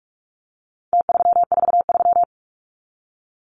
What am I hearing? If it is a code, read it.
Morse code, heard as a keyed tone, about 44 wpm, 700 Hz. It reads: T343
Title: Numbers - Contest Style - 4-Digit 45wpm